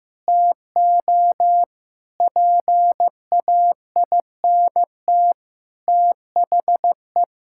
Morse code, heard as a keyed tone, about 15 wpm, 700 Hz. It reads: TO PAINT THE
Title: Sentences from Top 1000 Words - Review Part 6 - 15wpm